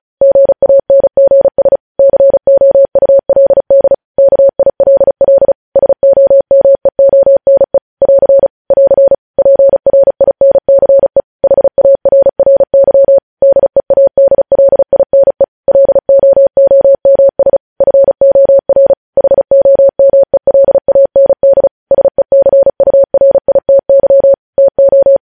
Morse code, heard as a keyed tone, about 36 wpm, 550 Hz. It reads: GANGS COULD KILL SOMEONE <AR> <AR> PRINCE HARRY DEADLINE LOOMS FOR HOMELAND SECURITY TO
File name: podcast / News headlines-1x-35x35